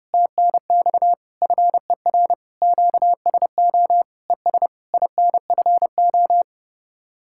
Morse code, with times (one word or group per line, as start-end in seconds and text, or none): 0.1-1.1 TNX
1.4-2.3 FER
2.6-4.0 QSO
4.3-4.7 ES
4.9-6.4 INFO